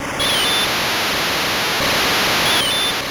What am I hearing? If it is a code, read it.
There is no code.